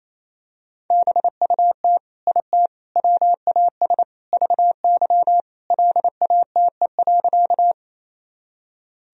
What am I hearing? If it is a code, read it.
Morse code, heard as a keyed tone, about 27 wpm, 700 Hz. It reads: BUT IT WAS VY LATE.
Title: Sentences from Top 300 Words and Abbreviations 28wpm